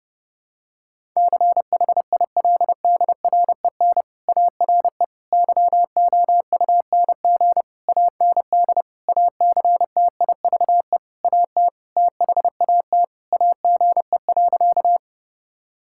Morse code, heard as a keyed tone, about 30 wpm, 700 Hz. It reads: CHILDREN ARE YOUNG AND ACTIVE AT THAT AGE.
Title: Sentences from Top 1000 Words - Review Part 6 - 30wpm